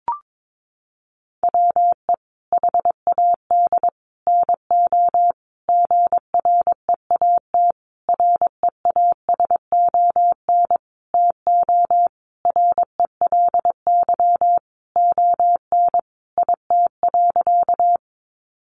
Morse code, heard as a keyed tone, about 22 wpm, 700 Hz. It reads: WE HAD NO GREAT REASON TO RELY ON IT.
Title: Sentences from Top 1200 Words 22wpm